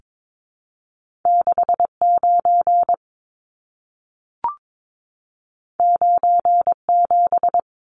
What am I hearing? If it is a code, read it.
Morse code, heard as a keyed tone, about 22 wpm, 700 Hz. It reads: 69 97
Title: Numbers - 2-Digit 22wpm